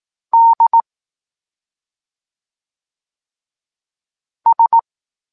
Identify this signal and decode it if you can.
Morse code, heard as a keyed tone, about 18 wpm, 920 Hz. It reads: D S
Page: Tekst 14 – doskonalenie odbioru poznanych liter